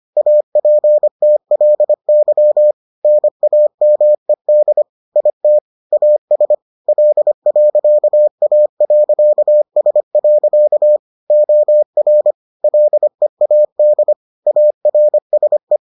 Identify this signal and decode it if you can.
Morse code, heard as a keyed tone, about 25 wpm, 600 Hz. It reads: APTLY NAMED IT AS L.A.S. OR LEAD ARSE